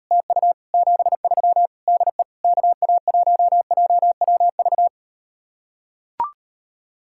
Morse code, heard as a keyed tone, about 38 wpm, 700 Hz. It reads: TU 73 DE KA1JWV